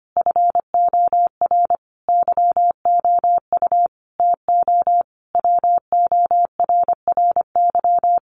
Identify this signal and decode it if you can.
Morse code, heard as a keyed tone, about 25 wpm, 700 Hz. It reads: FOR YOU TO WORRY
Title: Sentences from Top 1100 Words 25wpm